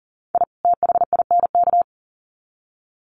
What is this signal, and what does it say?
Morse code, heard as a keyed tone, about 37 wpm, 700 Hz. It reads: I THINK
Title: Sets of 2 Words 40wpm